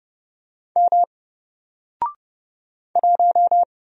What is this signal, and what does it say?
Morse code, heard as a keyed tone, about 30 wpm, 700 Hz. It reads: M 1
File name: Single-Letter-Number-30wpm